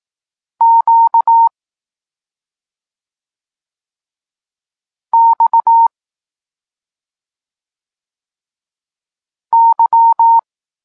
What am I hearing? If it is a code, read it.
Morse code, heard as a keyed tone, about 18 wpm, 920 Hz. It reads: Q X Y
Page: Tekst 32 – Doskonalenie odbioru poznanych liter